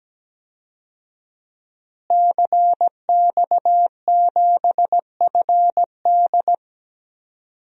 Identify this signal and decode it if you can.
Morse code, heard as a keyed tone, about 17 wpm, 700 Hz. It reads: CX7FD
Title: DXCC-100 Call Signs - 5 Characters - Repeated 2x - 17wpm